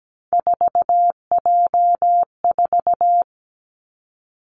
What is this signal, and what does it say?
Morse code, heard as a keyed tone, about 17 wpm, 700 Hz. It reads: 4J4
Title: Three Challenging Letter-Number Combinations 17wpm